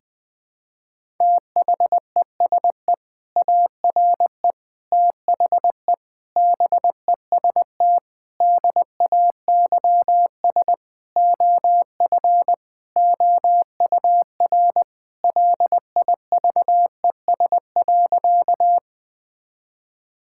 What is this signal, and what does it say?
Morse code, heard as a keyed tone, about 20 wpm, 700 Hz. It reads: THESE ARE THE BEST DAYS OF OUR LIVES.
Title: Sentences with 2-Word Phrases - 20wpm